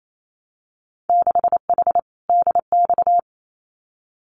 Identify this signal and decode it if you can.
Morse code, heard as a keyed tone, about 28 wpm, 700 Hz. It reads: 6H DX
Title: Winter Field Day Exchanges Encore 28wpm